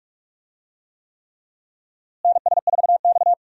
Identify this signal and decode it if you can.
Morse code, heard as a keyed tone, about 42 wpm, 700 Hz. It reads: NS4X
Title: USA Call Signs - 4 Characters - Repeated 3x - 45wpm